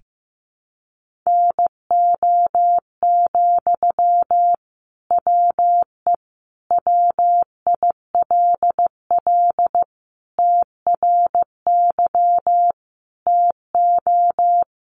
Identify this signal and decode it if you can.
Morse code, heard as a keyed tone, about 15 wpm, 700 Hz. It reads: NO, WE WILL TRY TO
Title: Sentences from Top 1100 Words 15wpm